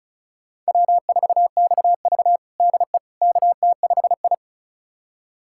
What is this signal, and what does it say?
Morse code, heard as a keyed tone, about 36 wpm, 700 Hz. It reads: W4XV DE KT5I
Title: USA Callsign DE Callsign - 4 Characters - 35wpm